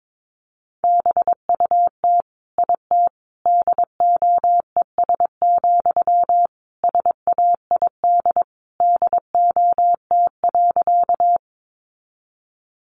Morse code, heard as a keyed tone, about 22 wpm, 700 Hz. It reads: BUT IT DOES, SAID DOT.
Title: Sentences from Top 1500 Words 22wpm